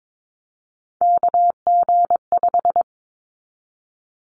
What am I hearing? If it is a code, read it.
Morse code, heard as a keyed tone, about 22 wpm, 700 Hz. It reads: KG5
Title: Callsign Prefixes for Top 25 DXCC - 22wpm